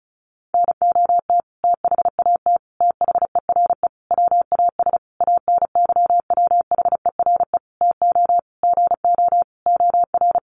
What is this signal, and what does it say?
Morse code, heard as a keyed tone, about 34 wpm, 700 Hz. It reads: NOT THAT THERE WAS ANYWHERE TO GO OR
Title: Sentences from Top 1400 Words 35wpm